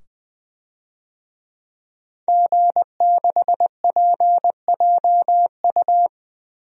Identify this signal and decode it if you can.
Morse code, heard as a keyed tone, about 21 wpm, 700 Hz. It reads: G6PJU
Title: DXCC-100 Call Signs - 5 Characters - 20wpm